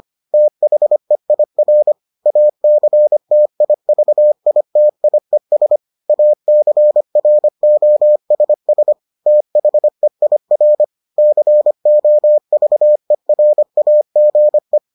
Morse code, heard as a keyed tone, about 25 wpm, 600 Hz. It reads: THEIR ACTIVITIES ACROSS THEIR COVERAGE